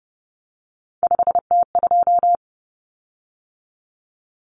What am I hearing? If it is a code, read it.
Morse code, heard as a keyed tone, about 30 wpm, 700 Hz. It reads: 5T2
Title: Common Cut Numbers - 3-Digit 30wpm